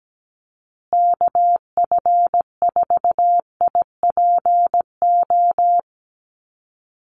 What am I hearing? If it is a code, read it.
Morse code, heard as a keyed tone, about 17 wpm, 700 Hz. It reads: KF4IPO
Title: USA Call Signs Repeated 2x 17wpm